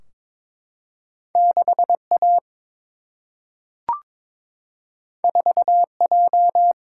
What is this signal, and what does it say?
Morse code, heard as a keyed tone, about 23 wpm, 700 Hz. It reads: 6A 4J
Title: Single-Letter Single-Number 22wpm